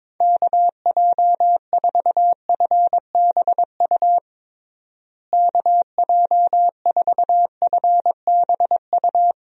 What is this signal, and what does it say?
Morse code, heard as a keyed tone, about 22 wpm, 700 Hz. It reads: KJ4FBU KJ4FBU
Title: USA Call Signs - 6 Characters - Repeated 2x - 22wpm